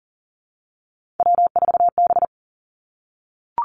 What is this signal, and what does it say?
Morse code, heard as a keyed tone, about 41 wpm, 700 Hz. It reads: W4B
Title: Three Challenging Letter-Number Combinations 40wpm